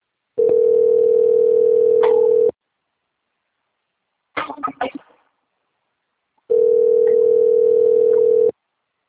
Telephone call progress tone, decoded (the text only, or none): ringback tone